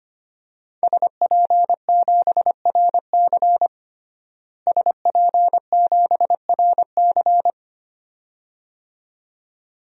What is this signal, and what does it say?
Morse code, heard as a keyed tone, about 25 wpm, 700 Hz. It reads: SP7RC SP7RC
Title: DXCC-100 Call Signs - 5 Characters - Repeated 2x - 25wpm